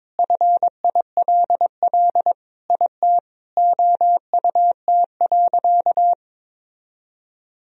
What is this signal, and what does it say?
Morse code, heard as a keyed tone, about 22 wpm, 700 Hz. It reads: FILL IT OUT.
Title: Sentences from Top 1000 Words - Review Part 3 - 22wpm